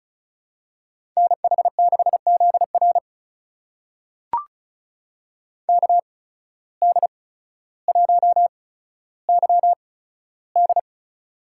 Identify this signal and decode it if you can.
Morse code, heard as a keyed tone, about 36 wpm, spaced as for 15 wpm, 700 Hz. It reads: NH6ZR KD1YD
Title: USA Call Signs - 5 Characters - 8x Character Spacing - 35wpm